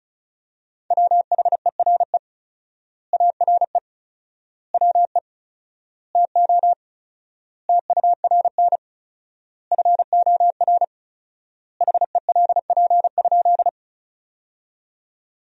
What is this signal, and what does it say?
Morse code, heard as a keyed tone, about 36 wpm, 700 Hz. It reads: WHERE ARE WE TO TURN FOR HELP?
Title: 4x Word-Spacing - Sentences from Top 200 Words - 35wpm